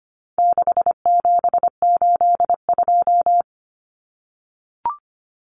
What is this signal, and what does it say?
Morse code, heard as a keyed tone, about 25 wpm, 700 Hz. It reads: 6782
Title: Common Cut Numbers - 4-Digit 25wpm